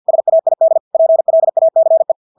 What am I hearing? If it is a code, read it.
Morse code, heard as a keyed tone, about 50 wpm, 650 Hz. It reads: SAID PLACE
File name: Engelske ord 50 wpm